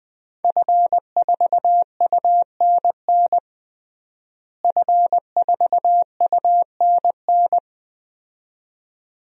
Morse code, heard as a keyed tone, about 20 wpm, 700 Hz. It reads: F4UNN F4UNN
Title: DXCC-100 Call Signs - 5 Characters - Repeated 2x - 20wpm